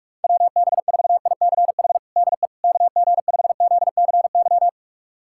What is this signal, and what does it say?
Morse code, heard as a keyed tone, about 46 wpm, 700 Hz. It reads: WB4ICH DE KG5ZCY
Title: USA Callsign DE Callsign - 6 Characters - 45wpm